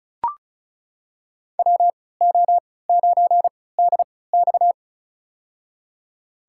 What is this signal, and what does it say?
Morse code, heard as a keyed tone, about 34 wpm, spaced as for 20 wpm, 700 Hz. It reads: WO9DX
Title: USA Call Signs - 3x Character Spacing - 35wpm